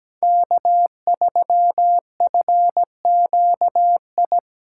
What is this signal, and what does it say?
Morse code, heard as a keyed tone, about 20 wpm, 700 Hz. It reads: K3FQI
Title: Speed Racing - USA 1x3 Call Signs 17wpm